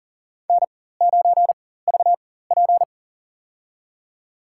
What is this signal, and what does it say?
Morse code, heard as a keyed tone, about 40 wpm, spaced as for 19 wpm, 700 Hz. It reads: N9VP
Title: USA Call Signs - 4 Characters - 4x Character Spacing - 40wpm